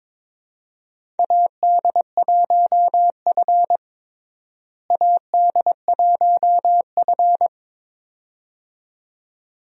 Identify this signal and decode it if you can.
Morse code, heard as a keyed tone, about 21 wpm, 700 Hz. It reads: AD1F AD1F
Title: USA Call Signs - 4 Characters - Repeated 2x - 22wpm